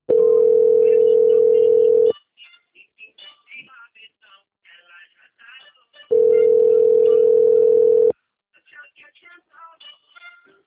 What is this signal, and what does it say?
Ringback tone (North America), 440 + 480 Hz.